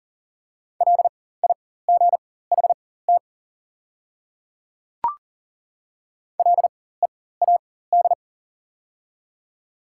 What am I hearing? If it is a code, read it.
Morse code, heard as a keyed tone, about 39 wpm, spaced as for 18 wpm, 690 Hz. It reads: LIGHT LEAD
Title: 4x Character Spacing - Top 500 Words - 40wpm